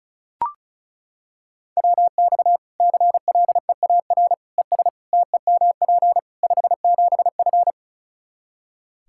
Morse code, heard as a keyed tone, about 36 wpm, 700 Hz. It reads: WX CLEAR ES TEMP 57F
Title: Practice QSO - Exchange 2 - Info - 35wpm